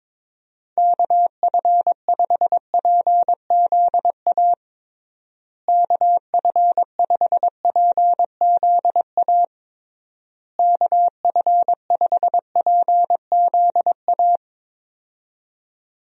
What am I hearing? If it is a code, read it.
Morse code, heard as a keyed tone, about 22 wpm, 700 Hz. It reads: KF5PZA KF5PZA KF5PZA
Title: USA Call Signs - 6 Characters - Repeated 3x - 22wpm